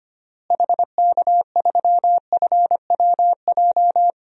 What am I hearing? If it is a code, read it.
Morse code, heard as a keyed tone, about 25 wpm, 700 Hz. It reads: HK3FWJ
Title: DXCC-100 Call Signs - 6 Characters - Repeated 2x - 25wpm